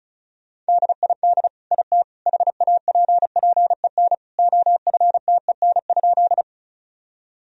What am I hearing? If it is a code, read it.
Morse code, heard as a keyed tone, about 34 wpm, 700 Hz. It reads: DID IT HAPPEN OFTEN?
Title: Sentences from Top 1000 Words - Review Part 1 - 35wpm